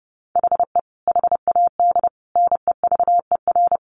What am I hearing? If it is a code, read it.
Morse code, heard as a keyed tone, about 28 wpm, 700 Hz. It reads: HE HAD NEVER
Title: Sentences from Top 1300 Words 30wpm